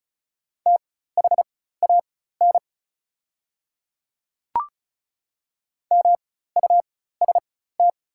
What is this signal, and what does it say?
Morse code, heard as a keyed tone, about 36 wpm, spaced as for 16 wpm, 700 Hz. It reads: THAN MUST
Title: 4x Character Spacing - Top 200 Words - 35wpm